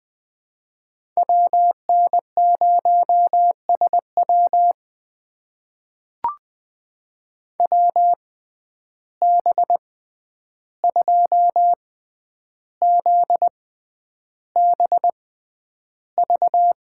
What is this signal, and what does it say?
Morse code, heard as a keyed tone, about 20 wpm, spaced as for 10 wpm, 700 Hz. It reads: WN0SW WB2ZBV
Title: USA Call Signs - 6x Character Spacing - 20wpm